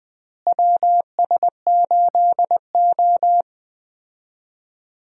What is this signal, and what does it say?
Morse code, heard as a keyed tone, about 21 wpm, 700 Hz. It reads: WS8O
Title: USA Call Signs - 4 Characters - Repeated 2x - 20wpm